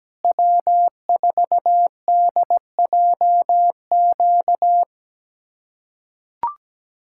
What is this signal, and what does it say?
Morse code, heard as a keyed tone, about 17 wpm, 700 Hz. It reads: W4DJQ
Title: USA Callsign DE Callsign - 5 Characters - 17wpm